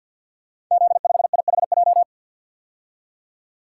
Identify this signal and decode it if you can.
Morse code, heard as a keyed tone, about 48 wpm, 700 Hz. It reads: Z5IHJ
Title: Code Groups - 5 Letter:Numbers - 50wpm